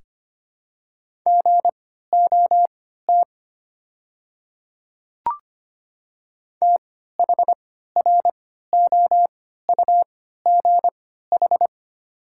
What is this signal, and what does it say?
Morse code, heard as a keyed tone, about 25 wpm, spaced as for 14 wpm, 700 Hz. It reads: GOT THROUGH
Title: 3x Character Spacing - Top 300 Words - 25wpm